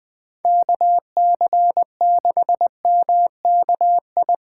Morse code, heard as a keyed tone, about 20 wpm, 700 Hz. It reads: KC6MKI
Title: USA Call Signs - 6 Characters - 4x Character Spacing - 20wpm